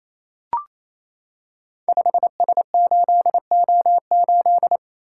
Morse code, heard as a keyed tone, about 27 wpm, 700 Hz. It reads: 5S8O8
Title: Code Groups - 5 Letter:Numbers - 28wpm